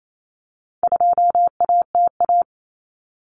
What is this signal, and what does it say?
Morse code, heard as a keyed tone, about 28 wpm, 700 Hz. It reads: 2ATA
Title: Common Cut Numbers - 4-Digit 28wpm